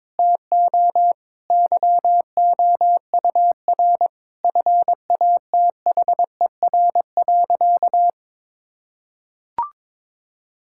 Morse code, heard as a keyed tone, about 22 wpm, 700 Hz. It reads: TO YOUR FATHER.
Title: Sentences from Top 1000 Words - Review Part 6 - 22wpm